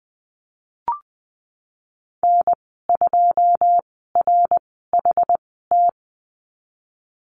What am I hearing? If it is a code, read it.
Morse code, heard as a keyed tone, about 20 wpm, spaced as for 14 wpm, 700 Hz. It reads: N2RHT
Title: 128 Lesson 21g - Number 6 - Call Signs 20wpm